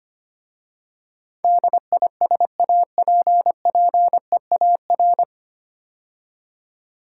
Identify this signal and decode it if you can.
Morse code, heard as a keyed tone, about 25 wpm, 690 Hz. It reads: DISAPPEAR
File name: practice / 25wpm / 9 Letter Words - First 3 Letters Given - 25wpm